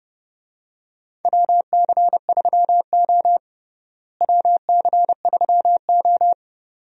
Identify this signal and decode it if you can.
Morse code, heard as a keyed tone, about 31 wpm, 700 Hz. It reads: WC3O WC3O
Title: USA Call Signs - 4 Characters - Repeated 2x - 30wpm